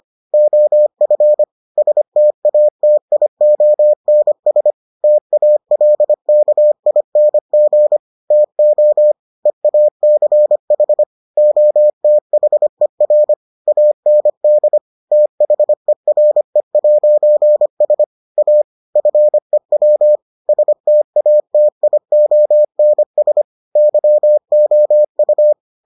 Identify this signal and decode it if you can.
Morse code, heard as a keyed tone, about 25 wpm, 600 Hz. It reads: OF STATIONS TALKING TO EACH OTHER AND THERE'S A FEW STATIONS YOU